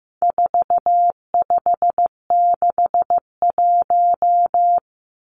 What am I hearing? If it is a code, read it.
Morse code, heard as a keyed tone, about 15 wpm, 700 Hz. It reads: 4561
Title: Numbers - 4-Digit 15wpm